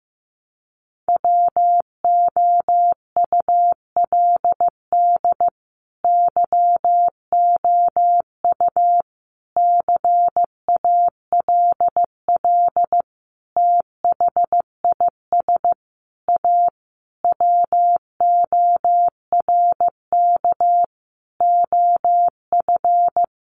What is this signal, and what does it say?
Morse code, heard as a keyed tone, about 15 wpm, 700 Hz. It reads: WOULD YOU CALL THIS A WORK OF
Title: Sentences from Top 600 Words 15wpm